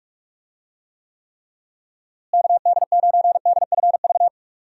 Morse code, heard as a keyed tone, about 46 wpm, 700 Hz. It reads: KD9DFV
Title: USA Call Signs - 6 Characters - Repeated 3x - 45wpm